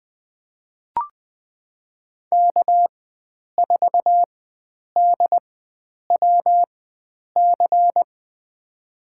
Morse code, heard as a keyed tone, about 21 wpm, spaced as for 9 wpm, 700 Hz. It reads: K4DWC
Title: USA Call Signs - 5 Characters - 4x Character Spacing - 20wpm